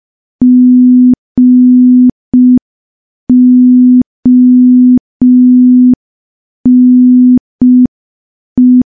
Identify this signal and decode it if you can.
Morse code, heard as a keyed tone, about 5 wpm, 250 Hz. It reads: GONE
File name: morse / morse-training-05WPM